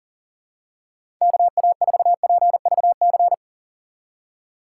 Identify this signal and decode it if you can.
Morse code, heard as a keyed tone, about 37 wpm, 700 Hz. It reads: KA4PVC
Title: USA Call Signs - 6 Characters - 3x Character Spacing - 40wpm